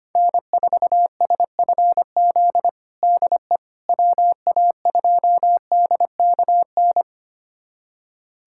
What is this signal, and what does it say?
Morse code, heard as a keyed tone, about 25 wpm, 700 Hz. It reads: N4SFZ DE WA2DKN
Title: USA Callsign DE Callsign 25wpm